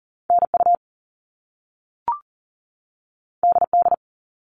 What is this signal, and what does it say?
Morse code, heard as a keyed tone, about 40 wpm, 700 Hz. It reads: NU DD
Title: All Cut Numbers - 2-Digit 40wpm